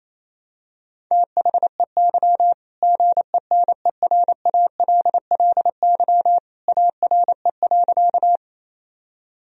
Morse code, heard as a keyed tone, about 28 wpm, 700 Hz. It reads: THEY GENERALLY ARE.
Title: Sentences from Top 1000 Words - Review Part 4 - 28wpm